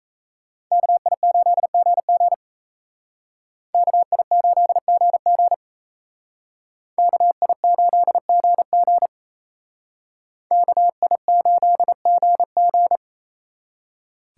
Morse code, heard as a keyed tone, about 32 wpm, 700 Hz. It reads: KI8GG KI8GG KI8GG KI8GG
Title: Speed Racing - USA 2x2 Call Signs 28wpm